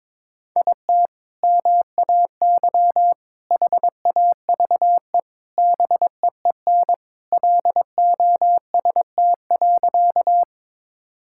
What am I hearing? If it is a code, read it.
Morse code, heard as a keyed tone, about 22 wpm, 700 Hz. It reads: IT MAY HAVE BEEN LOST.
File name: practice / 22wpm / Sentences with 3-Word Phrases - 22wpm